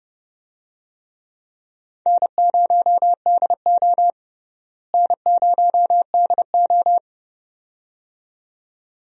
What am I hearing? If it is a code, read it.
Morse code, heard as a keyed tone, about 30 wpm, 700 Hz. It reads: N0DO N0DO
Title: USA Call Signs - 4 Characters - Repeated 2x - 30wpm